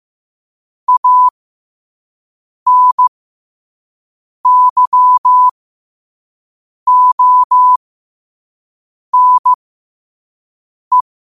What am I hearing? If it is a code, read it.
Morse code, heard as a keyed tone, about 15 wpm, spaced as for 5 wpm, 1000 Hz. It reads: ANYONE
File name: qotdfiles / 5wpm